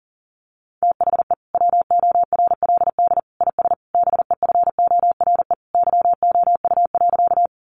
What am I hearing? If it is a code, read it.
Morse code, heard as a keyed tone, about 40 wpm, 700 Hz. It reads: THE WORLD IS BEFORE YOU.
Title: Sentences from Top 200 Words 40wpm